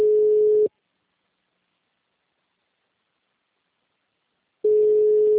Ringback tone (Europe, 425 Hz).
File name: out- -2025- - - .33313